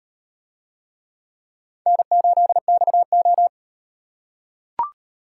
Morse code, heard as a keyed tone, about 38 wpm, 700 Hz. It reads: N8XO